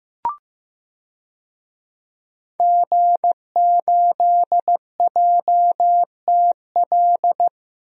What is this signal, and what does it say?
Morse code, heard as a keyed tone, about 15 wpm, 700 Hz. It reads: G8JTL
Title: DXCC-100 Call Signs - 5 Characters - 15wpm